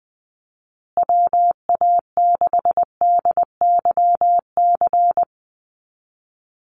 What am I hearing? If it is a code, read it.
Morse code, heard as a keyed tone, about 20 wpm, 700 Hz. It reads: WA6DYC
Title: USA Call Signs Repeated 3x 20wpm